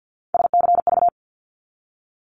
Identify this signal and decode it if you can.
Morse code, heard as a keyed tone, about 50 wpm, 700 Hz. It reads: SCV
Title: ARRL Sections 50wpm